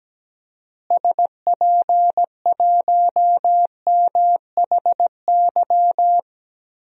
Morse code, heard as a keyed tone, about 17 wpm, 700 Hz. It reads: SP1MHY
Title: DXCC-100 Call Signs - 6 Characters - Repeated 3x - 17wpm